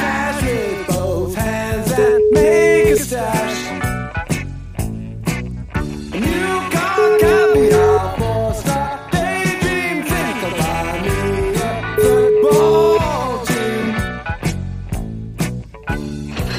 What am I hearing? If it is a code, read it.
Ringback tone (Europe, 425 Hz), with noise.